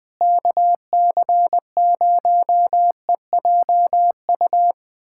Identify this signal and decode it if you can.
Morse code, heard as a keyed tone, about 20 wpm, 700 Hz. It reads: KC0EJU
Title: USA Callsign DE Callsign - 6 Characters - 20wpm